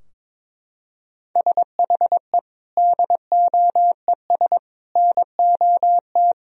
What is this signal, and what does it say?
Morse code, heard as a keyed tone, about 22 wpm, 700 Hz. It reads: SHE DOES NOT
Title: Sentences from Top 1000 Words - Review Part 4 - 22wpm